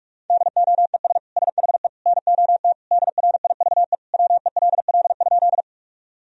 Morse code, heard as a keyed tone, about 45 wpm, 700 Hz. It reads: DOES SHE NOT DRIVE WELL?
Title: Sentences from Top 1000 Words - Review Part 2 - 45wpm